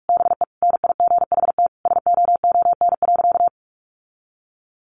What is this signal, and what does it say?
Morse code, heard as a keyed tone, about 44 wpm, 700 Hz. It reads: BE NIGHT SOON.
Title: Sentences from Top 1300 Words 45wpm